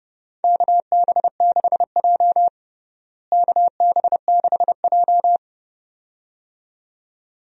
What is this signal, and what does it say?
Morse code, heard as a keyed tone, about 30 wpm, 700 Hz. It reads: KB6J KB6J
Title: USA Call Signs - 4 Characters - Repeated 2x - 30wpm